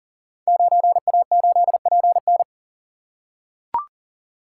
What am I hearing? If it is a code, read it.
Morse code, heard as a keyed tone, about 38 wpm, 700 Hz. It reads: 9A8PN
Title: Code Groups - 5 Letter:Numbers - 40wpm